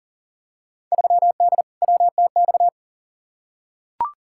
Morse code, heard as a keyed tone, about 40 wpm, 700 Hz. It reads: 3D WTX
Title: Field Day Exchanges Encore - 40wpm